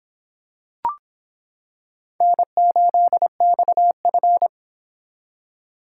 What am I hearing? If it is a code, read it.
Morse code, heard as a keyed tone, about 25 wpm, 700 Hz. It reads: N8XF